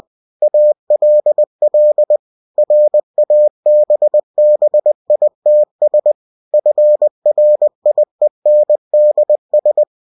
Morse code, heard as a keyed tone, about 20 wpm, 600 Hz. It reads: ALL RABBITS FRIENDS
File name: cw-books / winnie-the-pooh / winnie-20wpm-page-0106